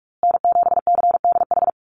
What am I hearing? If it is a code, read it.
Morse code, heard as a keyed tone, about 44 wpm, 690 Hz. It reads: N7CDH
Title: USA Call Signs Repeated 3x 45wpm